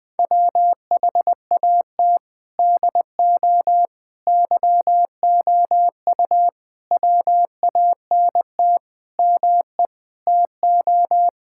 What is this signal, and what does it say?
Morse code, heard as a keyed tone, about 20 wpm, 700 Hz. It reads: WHAT DO YOU WANT ME TO
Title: Sentences with 7-Word Phrases - 20wpm